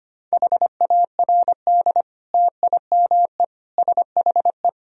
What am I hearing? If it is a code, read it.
Morse code, heard as a keyed tone, about 25 wpm, 700 Hz. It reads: HARD TIME SHE
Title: Sentences from Top 1000 Words - Review Part 1 - 25wpm